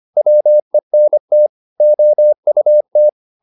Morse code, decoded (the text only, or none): WENT OUT